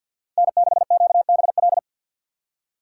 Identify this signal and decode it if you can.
Morse code, heard as a keyed tone, about 60 wpm, 700 Hz. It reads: N6QBL
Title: Speed Racing - USA 1x3 Call Signs 50wpm